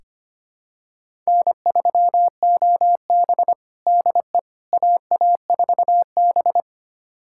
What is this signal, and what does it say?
Morse code, heard as a keyed tone, about 25 wpm, 700 Hz. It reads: N3OB DE AA4B
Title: USA Callsign DE Callsign - 4 Characters - 25wpm